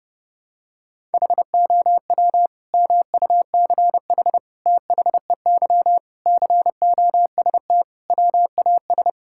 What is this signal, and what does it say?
Morse code, heard as a keyed tone, about 32 wpm, 700 Hz. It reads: HOW MUCH THEY COST WAS
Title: Sentences from Top 1000 Words - Review Part 5 - 30wpm